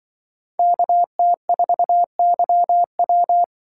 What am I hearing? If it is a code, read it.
Morse code, heard as a keyed tone, about 24 wpm, 700 Hz. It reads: KT4YW
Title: Speed Racing - USA 2x2 Call Signs 20wpm